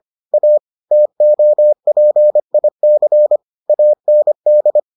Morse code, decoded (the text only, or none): A TOPIC AND